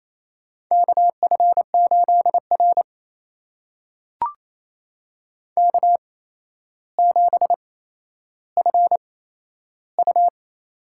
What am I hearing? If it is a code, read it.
Morse code, heard as a keyed tone, about 28 wpm, spaced as for 12 wpm, 700 Hz. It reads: KF8R K7FU
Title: USA Call Signs - 4 Characters - 8x Character Spacing - 28wpm